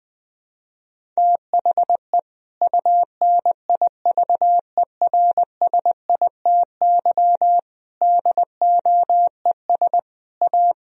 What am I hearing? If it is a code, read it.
Morse code, heard as a keyed tone, about 19 wpm, 700 Hz. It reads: THE UNIVERSITY DOES A
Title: Sentences from Top 1000 Words - Review Part 4 - 20wpm